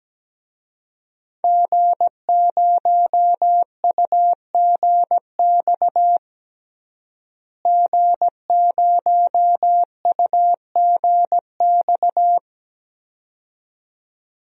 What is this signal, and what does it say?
Morse code, heard as a keyed tone, about 17 wpm, 700 Hz. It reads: G0UGX G0UGX
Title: DXCC-100 Call Signs - 5 Characters - Repeated 2x - 17wpm